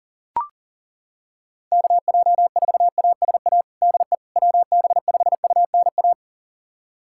Morse code, heard as a keyed tone, about 40 wpm, 700 Hz. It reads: KJ4ASA DE WB5UNA
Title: USA Callsign DE Callsign - 6 Characters - 40wpm